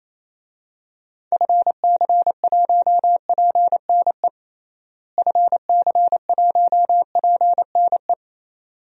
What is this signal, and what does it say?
Morse code, heard as a keyed tone, about 28 wpm, 700 Hz. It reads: FC1PNE FC1PNE
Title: DXCC-100 Call Signs - 6 Characters - Repeated 2x - 28wpm